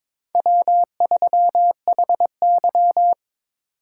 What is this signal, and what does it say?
Morse code, heard as a keyed tone, about 22 wpm, 700 Hz. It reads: W3HY
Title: USA Call Signs - 4x Character Spacing - 22wpm